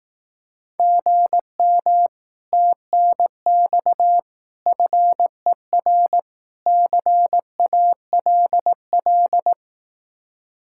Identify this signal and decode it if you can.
Morse code, heard as a keyed tone, about 18 wpm, 700 Hz. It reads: GM TNX FER CALL
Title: Speed-Racing Common QSO Phrases Encore - 15wpm